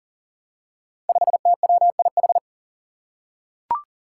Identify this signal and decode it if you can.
Morse code, heard as a keyed tone, about 38 wpm, 700 Hz. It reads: 5TWIH
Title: Five Challenging Letter-Number Combinations 40wpm